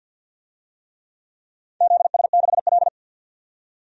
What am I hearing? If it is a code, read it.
Morse code, heard as a keyed tone, about 48 wpm, 700 Hz. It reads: ZS6L
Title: DXCC-100 Call Signs - 4 Characters - 50wpm